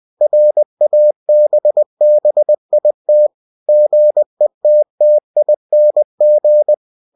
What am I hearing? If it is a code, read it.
Morse code, heard as a keyed tone, about 20 wpm, 600 Hz. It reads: RABBIT GETTING
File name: cw-books / winnie-the-pooh / winnie-20wpm-page-0015